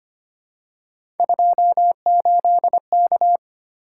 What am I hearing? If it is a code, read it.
Morse code, heard as a keyed tone, about 25 wpm, 700 Hz. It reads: 28K
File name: Three Letter-Number Combinations - 25wpm